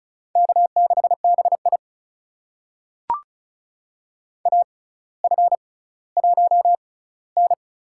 Morse code, heard as a keyed tone, about 36 wpm, spaced as for 18 wpm, 700 Hz. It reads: K6BI AF1N
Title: USA Call Signs - 4 Characters - 6x Character Spacing - 35wpm